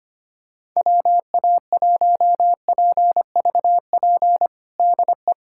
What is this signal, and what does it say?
Morse code, heard as a keyed tone, about 25 wpm, 700 Hz. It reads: WA1PVP DE
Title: USA Callsign DE Callsign - 6 Characters - 25wpm